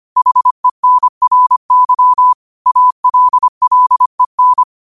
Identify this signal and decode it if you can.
Morse code, heard as a keyed tone, about 25 wpm, 1000 Hz. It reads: HENRY ALLEN
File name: qotdfiles / 25wpm